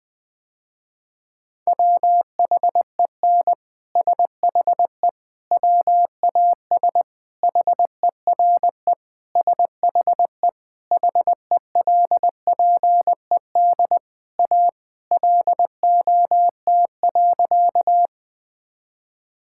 Morse code, heard as a keyed tone, about 20 wpm, 700 Hz. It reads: WHEN SHE WAS HERE SHE HELPED A LOT.
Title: Sentences with 3-Word Phrases - Spoken Beginning - 20wpm